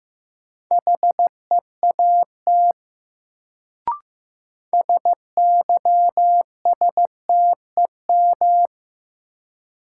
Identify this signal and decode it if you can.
Morse code, heard as a keyed tone, about 15 wpm, 700 Hz. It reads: HEAT SYSTEM
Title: Sentences from Top 1000 Words - Review Part 3 - New Word Drill - 15wpm